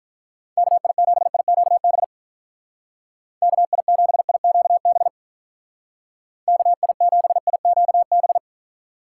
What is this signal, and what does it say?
Morse code, heard as a keyed tone, about 44 wpm, 700 Hz. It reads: KI7IQB KI7IQB KI7IQB